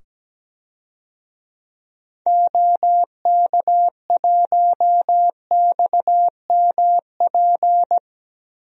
Morse code, heard as a keyed tone, about 17 wpm, 700 Hz. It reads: OK1XMP
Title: DXCC-100 Call Signs - 6 Characters - 17wpm